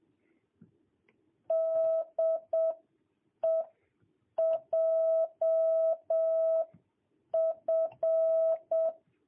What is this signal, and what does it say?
Morse code, heard as a keyed tone, about 7 wpm, spaced as for 6 wpm, 650 Hz. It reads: DE JF
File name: jh8cbh / 2022cw